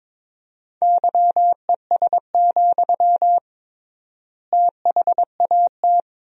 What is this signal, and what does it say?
Morse code, heard as a keyed tone, about 22 wpm, 700 Hz. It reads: YES, THAT
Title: 3x Word-Spacing - Sentences from Top 500 Words - 22wpm